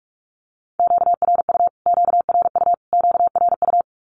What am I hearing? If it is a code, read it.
Morse code, heard as a keyed tone, about 44 wpm, 700 Hz. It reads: QRU QRU QRU
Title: Q-Codes Repeated 3x 45wpm